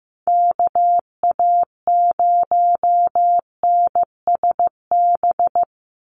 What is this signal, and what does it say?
Morse code, heard as a keyed tone, about 15 wpm, 700 Hz. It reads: KA0NSB
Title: USA Call Signs 15wpm